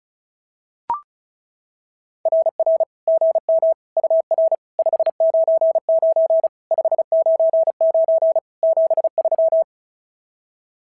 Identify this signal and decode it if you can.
Morse code, heard as a keyed tone, about 36 wpm, 660 Hz. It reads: RR GM UR 599 599 73
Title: SOTA QSO Practice 35wpm